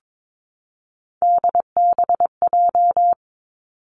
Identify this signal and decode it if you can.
Morse code, heard as a keyed tone, about 22 wpm, 700 Hz. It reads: DBJ
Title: Three Challenging Letter-Number Combinations 22wpm